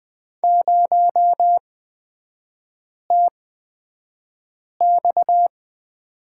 Morse code, heard as a keyed tone, about 20 wpm, spaced as for 5 wpm, 690 Hz. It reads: 0TX
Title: 229 Lesson 37e - Letter x - Three Characters 20wpm